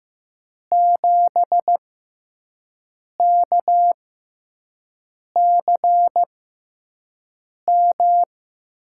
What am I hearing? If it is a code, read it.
Morse code, heard as a keyed tone, about 15 wpm, spaced as for 5 wpm, 700 Hz. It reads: 7KCM
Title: USA Call Signs - 6 Characters - 6x Character Spacing - 15wpm